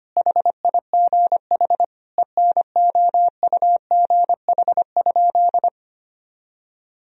Morse code, heard as a keyed tone, about 25 wpm, 700 Hz. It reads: HIGH ENOUGH?
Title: Sentences from Top 1000 Words - Review Part 4 - 25wpm